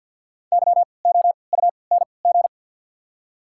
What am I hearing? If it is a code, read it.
Morse code, heard as a keyed tone, about 54 wpm, spaced as for 29 wpm, 690 Hz. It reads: YOUNG
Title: 3x Character Spacing - Top 500 Words - 50wpm